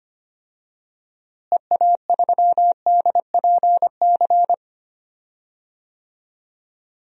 Morse code, heard as a keyed tone, about 25 wpm, 700 Hz. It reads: EA3DPC
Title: DXCC-100 Call Signs - 6 Characters - Repeated 3x - 25wpm